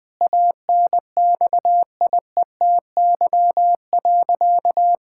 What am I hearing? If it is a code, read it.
Morse code, heard as a keyed tone, about 20 wpm, 700 Hz. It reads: ANXIETY.
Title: Sentences from Top 1600 Words 20wpm